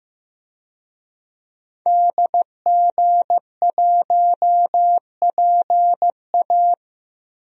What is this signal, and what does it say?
Morse code, heard as a keyed tone, about 15 wpm, 700 Hz. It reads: DG1PA
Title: DXCC-100 Call Signs - 5 Characters - Repeated 2x - 15wpm